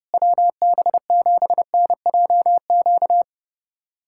Morse code, heard as a keyed tone, about 30 wpm, 700 Hz. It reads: WB7NJQ
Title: USA Callsign DE Callsign - 6 Characters - 30wpm